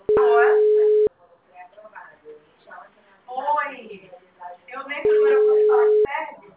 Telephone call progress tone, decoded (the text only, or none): ringback tone